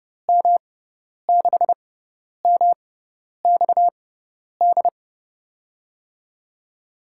Morse code, heard as a keyed tone, about 32 wpm, spaced as for 10 wpm, 700 Hz. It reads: M6MXD
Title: USA Call Signs - 6 Characters - 6x Character Spacing - 30wpm